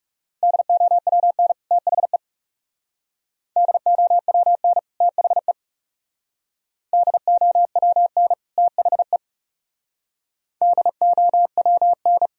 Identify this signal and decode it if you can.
Morse code, heard as a keyed tone, about 35 wpm, 700 Hz. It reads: DOWN THE DOWN THE DOWN THE DOWN
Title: Speed-Racing - Sets of 2 Words 30wpm